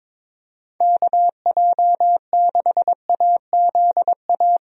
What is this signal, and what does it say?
Morse code, heard as a keyed tone, about 22 wpm, 700 Hz. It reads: KJ6AZA
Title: USA Call Signs - 6 Characters - Repeated 3x - 22wpm